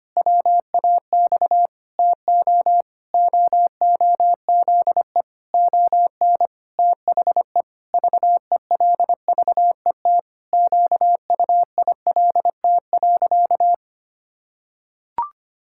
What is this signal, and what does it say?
Morse code, heard as a keyed tone, about 25 wpm, 700 Hz. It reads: WAX TO OOZE ON THE VELVET QUILT.
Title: Pangram Challenge v2 - 25wpm